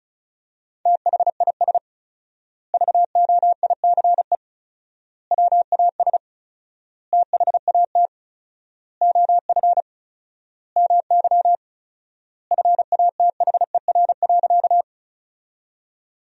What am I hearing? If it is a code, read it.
Morse code, heard as a keyed tone, about 34 wpm, 700 Hz. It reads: THIS VOICE WAS THAT OF MY FATHER.
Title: 4x Word-Spacing - Sentences from Top 500 Words - 35wpm